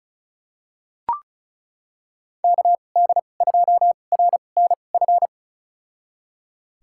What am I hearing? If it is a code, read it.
Morse code, heard as a keyed tone, about 36 wpm, spaced as for 25 wpm, 700 Hz. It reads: KD2RNF
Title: USA Call Signs - 6 Characters - 2x Character Spacing - 35wpm